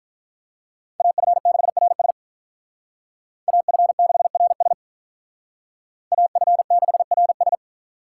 Morse code, heard as a keyed tone, about 46 wpm, 700 Hz. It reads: AF6RS AF6RS AF6RS